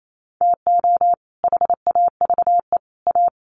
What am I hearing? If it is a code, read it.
Morse code, heard as a keyed tone, about 28 wpm, 700 Hz. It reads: TO HAVE A
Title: Sets of 3 Words 28wpm